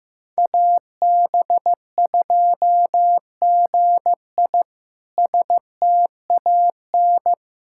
Morse code, heard as a keyed tone, about 15 wpm, 700 Hz. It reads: AB2GI STAN
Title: K1USN SST QSO Practice - 15wpm